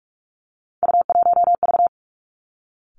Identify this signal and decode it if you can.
Morse code, heard as a keyed tone, about 46 wpm, 700 Hz. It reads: U1V
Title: Three Challenging Letter-Number Combinations 45wpm